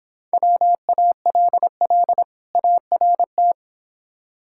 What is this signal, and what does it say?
Morse code, heard as a keyed tone, about 25 wpm, 700 Hz. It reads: WALL ART
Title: Difficult Words - Speed-Racing with Hints - 20wpm